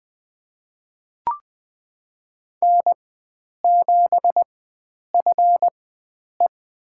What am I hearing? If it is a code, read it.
Morse code, heard as a keyed tone, about 20 wpm, spaced as for 9 wpm, 700 Hz. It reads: N7FE
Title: USA Call Signs - 4x Character Spacing - 20wpm